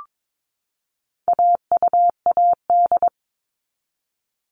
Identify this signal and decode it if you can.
Morse code, heard as a keyed tone, about 22 wpm, 700 Hz. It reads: AUAD
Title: All Cut Numbers - 4-Digit 22wpm